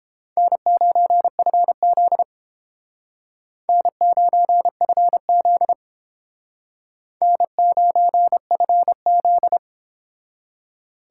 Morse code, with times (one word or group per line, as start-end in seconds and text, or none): 0.4-2.2 N9FZ
3.7-5.7 N9FZ
7.2-9.6 N9FZ